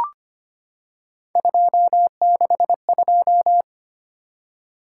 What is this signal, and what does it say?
Morse code, heard as a keyed tone, about 25 wpm, 700 Hz. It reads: 262
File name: Three Letter-Number Combinations - 25wpm